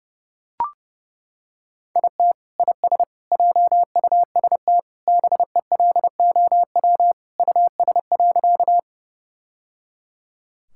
Morse code, heard as a keyed tone, about 30 wpm, 700 Hz. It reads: IT IS JUST BELOW US.
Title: Sentences from Top 1000 Words - Review Part 5 - 30wpm